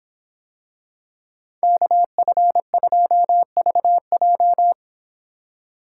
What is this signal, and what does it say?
Morse code, heard as a keyed tone, about 26 wpm, 700 Hz. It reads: KF2VJ